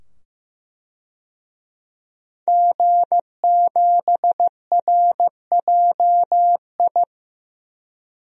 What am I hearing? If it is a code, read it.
Morse code, heard as a keyed tone, about 15 wpm, 700 Hz. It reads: G7RJI
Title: DXCC-100 Call Signs - 5 Characters - 15wpm